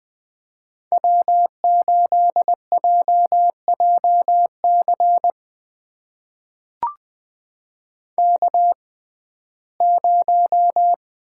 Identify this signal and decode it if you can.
Morse code, heard as a keyed tone, about 20 wpm, spaced as for 14 wpm, 700 Hz. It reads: W8JJC K0
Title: USA Call Signs - 5 Characters - 6x Character Spacing - 20wpm